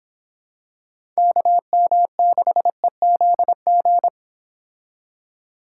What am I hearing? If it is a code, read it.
Morse code, heard as a keyed tone, about 26 wpm, 700 Hz. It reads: KM6EZG
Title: Speed Racing - USA 2x3 Call Signs 22wpm